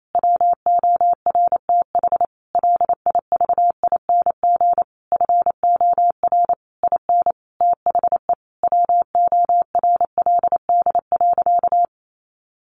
Morse code, heard as a keyed tone, about 28 wpm, 700 Hz. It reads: WORTH LIVING FOR IN THE WORLD.
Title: Sentences from Top 1100 Words 28wpm